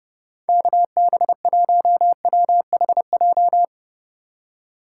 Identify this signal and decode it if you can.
Morse code, heard as a keyed tone, about 29 wpm, 700 Hz. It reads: KB1WHJ